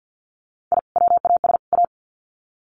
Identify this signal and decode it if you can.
Morse code, heard as a keyed tone, about 50 wpm, 700 Hz. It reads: I WAS A
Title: Sets of 3 Words 50wpm